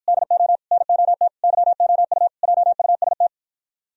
Morse code, heard as a keyed tone, about 53 wpm, 700 Hz. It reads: DO NOT YOU JUST